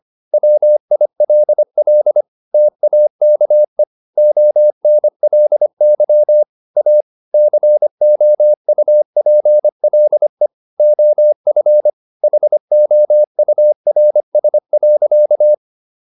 Morse code, heard as a keyed tone, about 25 wpm, 600 Hz. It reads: WILL TAKE ONLY A COUPLE OF HOURS.